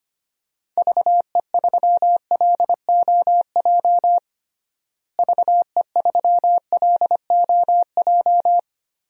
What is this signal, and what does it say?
Morse code, heard as a keyed tone, about 25 wpm, 700 Hz. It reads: VE3LOJ VE3LOJ
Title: DXCC-100 Call Signs - 6 Characters - Repeated 3x - 25wpm